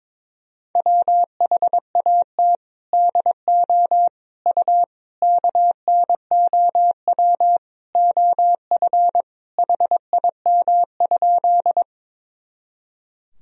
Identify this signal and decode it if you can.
Morse code, heard as a keyed tone, about 21 wpm, 700 Hz. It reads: WHAT DO U KNOW OF HIM?
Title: Sentences from Top 100 Words and Abbreviations 22wpm